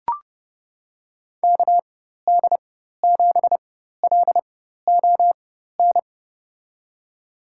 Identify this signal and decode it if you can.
Morse code, heard as a keyed tone, about 31 wpm, spaced as for 14 wpm, 700 Hz. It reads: KD7LON
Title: USA Call Signs - 4x Character Spacing - 30wpm